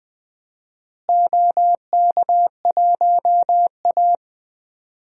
Morse code, heard as a keyed tone, about 20 wpm, 700 Hz. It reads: OK1A
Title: DXCC-100 Call Signs - 4 Characters - Repeated 2x - 20wpm